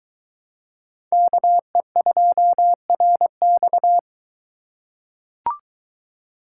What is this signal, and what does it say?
Morse code, heard as a keyed tone, about 21 wpm, 700 Hz. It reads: KE2RX